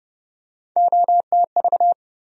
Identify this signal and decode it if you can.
Morse code, heard as a keyed tone, about 30 wpm, 700 Hz. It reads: OTV
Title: USA Call Sign Postfixes - 3 Letters 30wpm